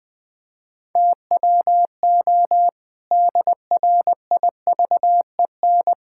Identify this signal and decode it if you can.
Morse code, heard as a keyed tone, about 20 wpm, 700 Hz. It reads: TWO DRIVEN
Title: Pangram Challenge v2 - 20wpm